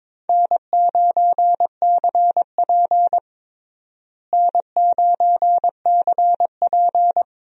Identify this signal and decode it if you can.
Morse code, heard as a keyed tone, about 21 wpm, 700 Hz. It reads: N9CP N9CP
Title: USA Call Signs - 4 Characters - Repeated 2x - 22wpm